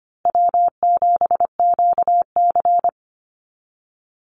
Morse code, heard as a keyed tone, about 25 wpm, 700 Hz. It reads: W7QC
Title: Call Signs Repeated 3x - 25wpm